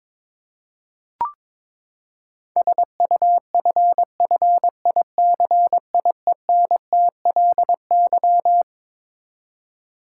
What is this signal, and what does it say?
Morse code, heard as a keyed tone, about 22 wpm, 700 Hz. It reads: SUFFICIENTLY
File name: Twelve Letter Words - 22wpm